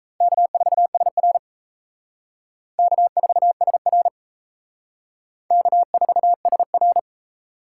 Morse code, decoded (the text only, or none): K4SR K4SR K4SR